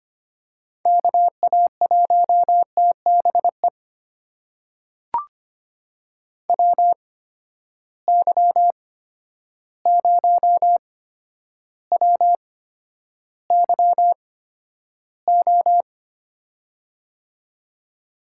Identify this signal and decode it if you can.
Morse code, heard as a keyed tone, about 25 wpm, spaced as for 11 wpm, 700 Hz. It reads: KA1TBE WY0WYO
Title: USA Call Signs - 6 Characters - 8x Character Spacing - 25wpm